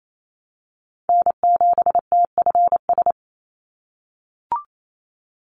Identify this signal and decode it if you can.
Morse code, heard as a keyed tone, about 28 wpm, 700 Hz. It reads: N7TFS
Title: USA Call Signs 28wpm